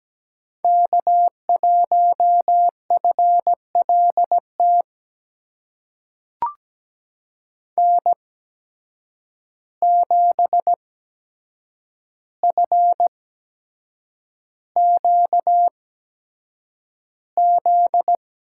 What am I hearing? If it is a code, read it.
Morse code, heard as a keyed tone, about 17 wpm, spaced as for 7 wpm, 700 Hz. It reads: K1FLT N7FQZ
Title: USA Call Signs - 5 Characters - 8x Character Spacing - 17wpm